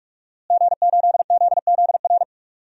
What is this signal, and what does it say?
Morse code, heard as a keyed tone, about 43 wpm, 700 Hz. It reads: G8ZZR
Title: DXCC-100 Call Signs - 5 Characters - Repeated 3x - 45wpm